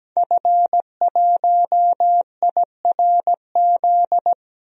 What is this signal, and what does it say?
Morse code, heard as a keyed tone, about 17 wpm, 700 Hz. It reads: F1IRZ
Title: DXCC-100 Call Signs - 5 Characters - Repeated 3x - 17wpm